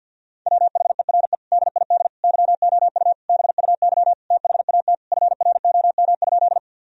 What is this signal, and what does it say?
Morse code, heard as a keyed tone, about 50 wpm, 700 Hz. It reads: WHERE DID YOU BUY THAT FROM?
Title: Sentences from Top 1000 Words - Review Part 4 - 50wpm